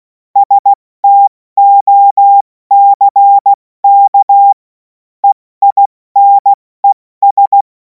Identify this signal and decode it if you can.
Morse code, heard as a keyed tone, about 16 wpm, 800 Hz. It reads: STOCK EINES
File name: Morsekurs / Witze / Witz 11. August 2026 70 F80